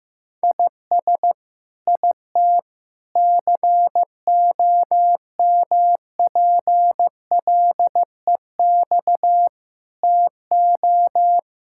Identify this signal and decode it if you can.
Morse code, heard as a keyed tone, about 15 wpm, 700 Hz. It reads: IS IT COMPLEX TO
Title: Sentences from Top 1000 Words - Review Part 6 - 15wpm